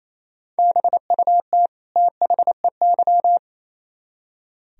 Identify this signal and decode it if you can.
Morse code, heard as a keyed tone, about 36 wpm, 700 Hz. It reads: BUT THEY